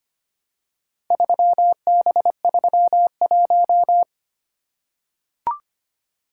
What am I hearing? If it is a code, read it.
Morse code, heard as a keyed tone, about 25 wpm, 690 Hz. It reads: 3B31
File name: practice / 25wpm / Four Letter-Number Combinations - 25wpm